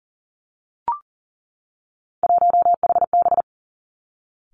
Morse code, heard as a keyed tone, about 41 wpm, 690 Hz. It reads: 1HB
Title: Three Challenging Letter-Number Combinations 40wpm